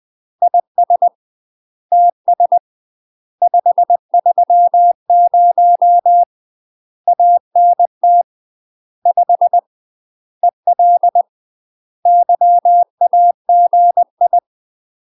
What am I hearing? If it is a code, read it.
Morse code, heard as a keyed tone, about 20 wpm, 700 Hz. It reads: IS TS 530 ANT 5 EL YAGI
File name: UK MASTER QSOs TRAINING / 20WPM / UK MASTER QSOs 0063